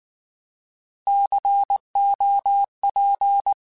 Morse code, heard as a keyed tone, about 19 wpm, 800 Hz. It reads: COP